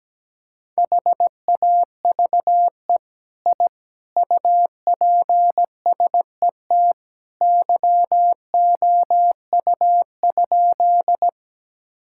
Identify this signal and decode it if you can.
Morse code, heard as a keyed tone, about 17 wpm, 700 Hz. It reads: HAVE I UPSET YOU?
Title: Sentences from Top 1000 Words - Review Part 5 - 17wpm